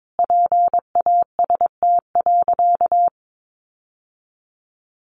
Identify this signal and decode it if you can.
Morse code, heard as a keyed tone, about 22 wpm, 700 Hz. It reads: PAST.
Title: Sentences from Top 1200 Words 22wpm